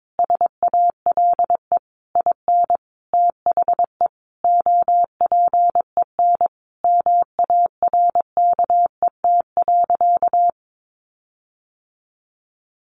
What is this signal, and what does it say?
Morse code, heard as a keyed tone, about 22 wpm, 700 Hz. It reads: SALE IN THE OPEN MARKET.
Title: Sentences from Top 1400 Words 22wpm